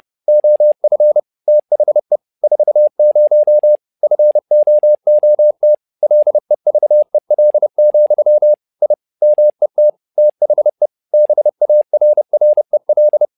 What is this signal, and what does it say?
Morse code, heard as a keyed tone, about 30 wpm, 600 Hz. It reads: OF THE 40 FOOT LEVEL, I MET THE BARREL
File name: learning center / tower trouble / tower trouble 30wpm